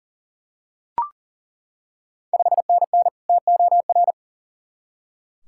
Morse code, heard as a keyed tone, about 37 wpm, 700 Hz. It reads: 5NN TOR
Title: Ontario QSO Party Exchange 40wpm